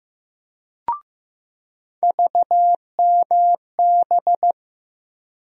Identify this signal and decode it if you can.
Morse code, heard as a keyed tone, about 15 wpm, 700 Hz. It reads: VMB
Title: USA Call Sign Postfixes - 3 Letters 15wpm